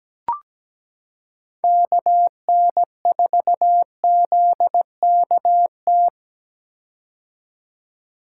Morse code, heard as a keyed tone, about 17 wpm, 700 Hz. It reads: KN4ZKT
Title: K1USN SST QSO Practice - 17wpm